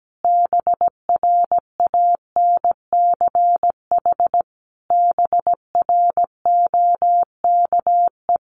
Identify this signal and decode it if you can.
Morse code, heard as a keyed tone, about 17 wpm, 700 Hz. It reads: BRANCH BROKE
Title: A Sound Of Thunder 17wpm